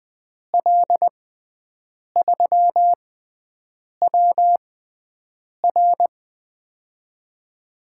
Morse code, heard as a keyed tone, about 20 wpm, spaced as for 7 wpm, 700 Hz. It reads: L3WR
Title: USA Call Signs - 6x Character Spacing - 20wpm